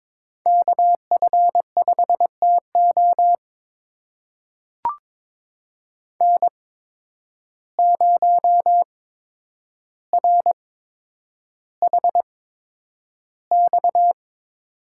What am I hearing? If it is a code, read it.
Morse code, heard as a keyed tone, about 22 wpm, spaced as for 9 wpm, 700 Hz. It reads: KF5TO N0RHX
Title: USA Call Signs - 5 Characters - 8x Character Spacing - 22wpm